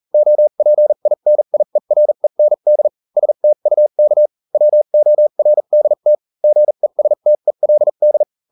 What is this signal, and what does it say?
Morse code, heard as a keyed tone, about 40 wpm, spaced as for 35 wpm, 600 Hz. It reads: OPINIEREND STUK WORDT GESTELD